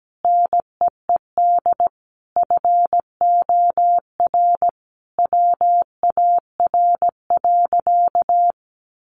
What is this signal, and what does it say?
Morse code, heard as a keyed tone, about 17 wpm, 700 Hz. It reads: NEED FOR WAR.
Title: Sentences from Top 1300 Words 17wpm